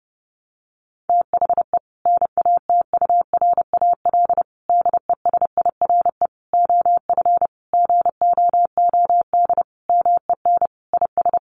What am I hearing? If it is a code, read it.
Morse code, heard as a keyed tone, about 28 wpm, 700 Hz. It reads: THE NATURAL DESIRE OF GOOD MEN IS
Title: Sentences from Top 1100 Words 30wpm